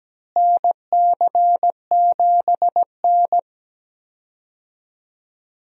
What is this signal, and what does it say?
Morse code, heard as a keyed tone, about 17 wpm, 700 Hz. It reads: NC7N
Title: USA Call Signs - 4 Characters - Repeated 3x - 17wpm